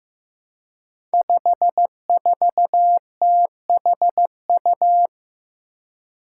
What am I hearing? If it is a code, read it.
Morse code, heard as a keyed tone, about 15 wpm, 700 Hz. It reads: 54THU
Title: Five Challenging Letter-Number Combinations 15wpm